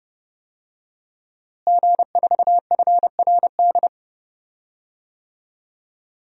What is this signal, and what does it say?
Morse code, heard as a keyed tone, about 30 wpm, 700 Hz. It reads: G4FRD
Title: DXCC-100 Call Signs - 5 Characters - Repeated 2x - 30wpm